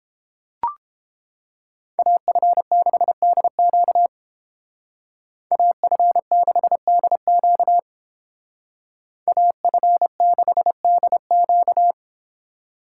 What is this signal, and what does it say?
Morse code, heard as a keyed tone, about 31 wpm, 700 Hz. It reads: AF6DQ AF6DQ AF6DQ